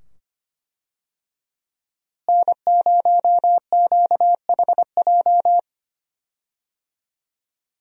Morse code, heard as a keyed tone, about 25 wpm, 700 Hz. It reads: N0QHJ
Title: USA Call Signs - 5 Characters - Repeated 2x - 25wpm